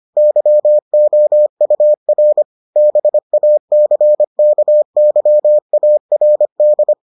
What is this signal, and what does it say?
Morse code, heard as a keyed tone, about 25 wpm, 600 Hz. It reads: YOUR BACKYARD